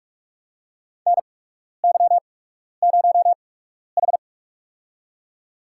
Morse code, heard as a keyed tone, about 43 wpm, spaced as for 12 wpm, 700 Hz. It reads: NY0H
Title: USA Call Signs - 4 Characters - 8x Character Spacing - 45wpm